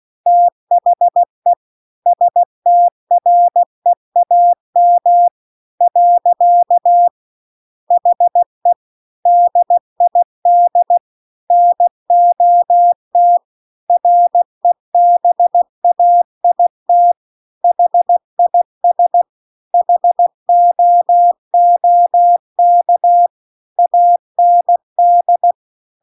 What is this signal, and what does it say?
Morse code, heard as a keyed tone, about 16 wpm, 700 Hz. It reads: THE STREAM . HE DID NOT REBAIT HIS HOOK AND
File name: CW-The-Big-Twoheaded-River-2-16WPM